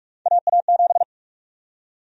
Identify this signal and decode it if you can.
Morse code, heard as a keyed tone, about 46 wpm, 690 Hz. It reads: AA7
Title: USA Call Sign Prefixes - Repeated 2x - 45wpm